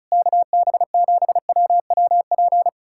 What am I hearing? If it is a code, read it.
Morse code, heard as a keyed tone, about 38 wpm, 700 Hz. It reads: KB7WWP
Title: Speed-Racing Common QSO Phrases Encore - 30wpm